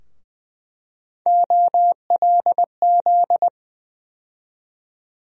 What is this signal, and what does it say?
Morse code, heard as a keyed tone, about 20 wpm, 700 Hz. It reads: OLZ
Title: USA Call Sign Postfixes - 3 Letters 20wpm